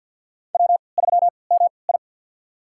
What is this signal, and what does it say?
Morse code, heard as a keyed tone, about 50 wpm, spaced as for 28 wpm, 700 Hz. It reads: W3MI
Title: USA Call Signs - 3x Character Spacing - 50wpm